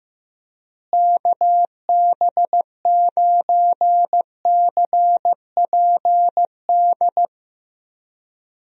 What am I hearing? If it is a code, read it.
Morse code, heard as a keyed tone, about 15 wpm, 700 Hz. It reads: KB9CPD
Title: USA Call Signs - 6 Characters - Repeated 3x - 15wpm